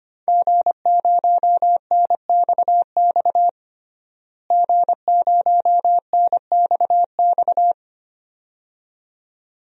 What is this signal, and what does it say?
Morse code, heard as a keyed tone, about 25 wpm, 700 Hz. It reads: G0NXX G0NXX
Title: DXCC-100 Call Signs - 5 Characters - Repeated 2x - 25wpm